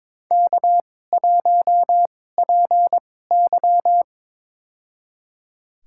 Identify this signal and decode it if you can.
Morse code, heard as a keyed tone, about 22 wpm, spaced as for 16 wpm, 700 Hz. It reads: K1PY
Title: USA Call Signs - 2x Character Spacing - 22wpm